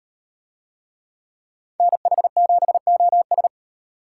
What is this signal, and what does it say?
Morse code, heard as a keyed tone, about 38 wpm, 700 Hz. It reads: NH7OS